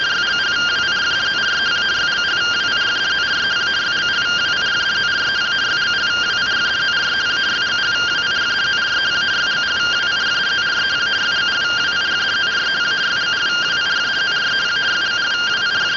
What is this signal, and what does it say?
Two-tone teleprinter signal, about 170 Hz shift.